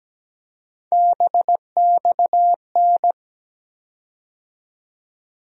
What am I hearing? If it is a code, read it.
Morse code, heard as a keyed tone, about 17 wpm, 700 Hz. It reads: BXN
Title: USA Call Sign Postfixes - 3 Letters 17wpm